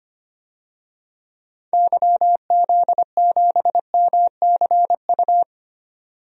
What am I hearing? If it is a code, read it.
Morse code, heard as a keyed tone, about 25 wpm, 700 Hz. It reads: YZ7MCU
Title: DXCC-100 Call Signs - 6 Characters - 25wpm